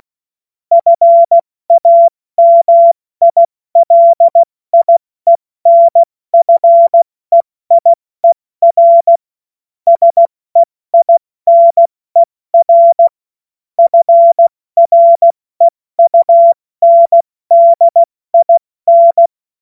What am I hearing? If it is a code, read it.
Morse code, heard as a keyed tone, about 16 wpm, 680 Hz. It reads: FAMILIENFEIER SEINER FREUNDIN
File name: Witz-05-August-2025-70-F80